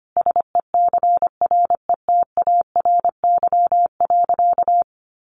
Morse code, heard as a keyed tone, about 25 wpm, 700 Hz. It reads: SECRETARY.
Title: Sentences from Top 1300 Words 25wpm